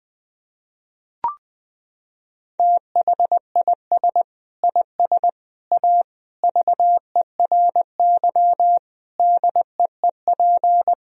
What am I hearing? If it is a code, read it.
Morse code, heard as a keyed tone, about 20 wpm, 700 Hz. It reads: THIS IS A VERY DEEP
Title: Sentences from Top 1000 Words - Review Part 3 - 20wpm